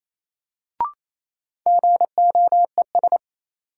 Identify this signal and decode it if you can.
Morse code, heard as a keyed tone, about 28 wpm, 700 Hz. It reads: GOES
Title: Sentences from Top 1500 Words - New Word Drill Alt - 28wpm